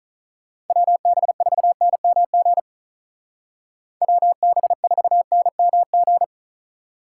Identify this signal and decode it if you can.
Morse code, heard as a keyed tone, about 43 wpm, 700 Hz. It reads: WB4NMG WB4NMG
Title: Speed Racing - USA 2x3 Call Signs 35wpm